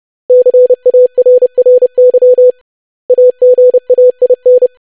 Morse code, heard as a keyed tone, about 30 wpm, 500 Hz. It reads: CARRY AGAIN
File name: lesson / web / basic500-25wpm